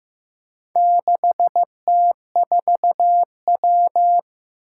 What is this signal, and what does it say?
Morse code, heard as a keyed tone, about 15 wpm, 700 Hz. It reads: 6T4W
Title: Four Challenging Letter-Number Combinations 15wpm